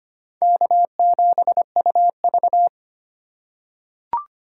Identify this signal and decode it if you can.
Morse code, heard as a keyed tone, about 25 wpm, 700 Hz. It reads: K7UV
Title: USA Call Signs - 2x Character Spacing - 25wpm